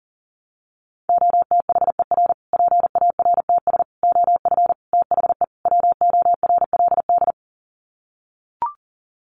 Morse code, heard as a keyed tone, about 40 wpm, 700 Hz. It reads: OTHER PARTS OF THE WORLD
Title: Sets of 5 Words Encore 40wpm